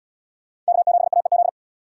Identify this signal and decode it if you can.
Morse code, heard as a keyed tone, about 73 wpm, 700 Hz. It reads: K7AEZ